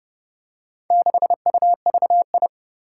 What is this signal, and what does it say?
Morse code, heard as a keyed tone, about 30 wpm, 700 Hz. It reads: 6UVI
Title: Four Challenging Letter-Number Combinations 30wpm